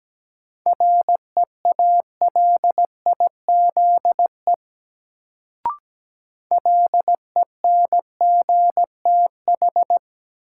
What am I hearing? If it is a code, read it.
Morse code, heard as a keyed tone, about 17 wpm, 700 Hz. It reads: REALIZE LENGTH
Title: Sentences from Top 1000 Words - New Word Drill - 17wpm